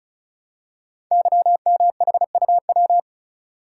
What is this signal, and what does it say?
Morse code, heard as a keyed tone, about 36 wpm, 700 Hz. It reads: YMHUW
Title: Code Groups - 5 Letter:Numbers - 35wpm